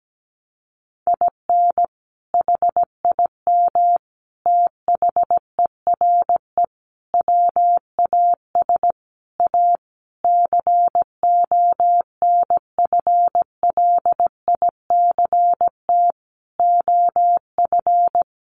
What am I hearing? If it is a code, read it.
Morse code, heard as a keyed tone, about 17 wpm, 700 Hz. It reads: IN HIM THERE WAS A CONFLICT OF
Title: Sentences from Top 1300 Words 17wpm